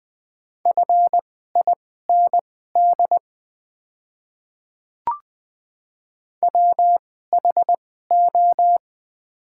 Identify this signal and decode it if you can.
Morse code, heard as a keyed tone, about 20 wpm, spaced as for 14 wpm, 700 Hz. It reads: FIND WHO
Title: 2x Character Spacing - Top 300 Words - 20wpm